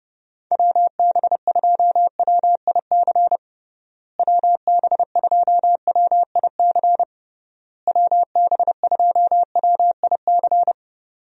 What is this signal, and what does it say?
Morse code, heard as a keyed tone, about 30 wpm, 700 Hz. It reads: WB2WIC WB2WIC WB2WIC
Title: USA Call Signs - 6 Characters - Repeated 3x - 30wpm